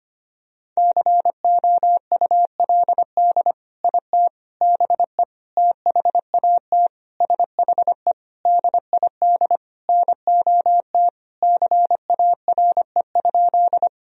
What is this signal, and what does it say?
Morse code, heard as a keyed tone, about 25 wpm, 700 Hz. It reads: COULD IT BE THAT SHE DID NOT CARE?
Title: Sentences from Top 1000 Words - Review Part 1 - 25wpm